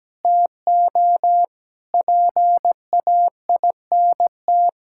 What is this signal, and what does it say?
Morse code, heard as a keyed tone, about 17 wpm, 700 Hz. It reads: TO PAINT
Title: Sentences from Top 1000 Words - Review Part 6 - 17wpm